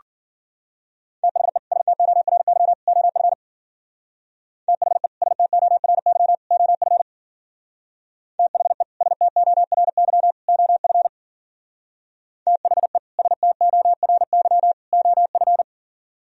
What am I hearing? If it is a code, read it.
Morse code, heard as a keyed tone, about 48 wpm, 700 Hz. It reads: THE STORY OF THE STORY OF THE STORY OF THE STORY OF
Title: Speed-Racing - Sets of 3 Words 40wpm